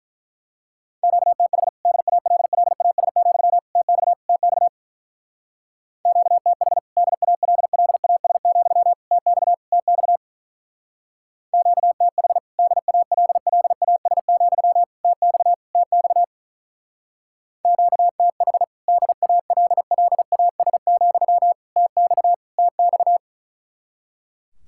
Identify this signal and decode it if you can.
Morse code, heard as a keyed tone, about 43 wpm, 700 Hz. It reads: QTH DALLAS, TX TX QTH DALLAS, TX TX QTH DALLAS, TX TX QTH DALLAS, TX TX
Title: Speed-Racing Common QSO Phrases - 35wpm